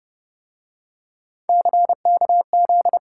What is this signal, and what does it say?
Morse code, heard as a keyed tone, about 29 wpm, 700 Hz. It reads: CKZ
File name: Speed-Racing - Three Letter-Number Combinations - 25wpm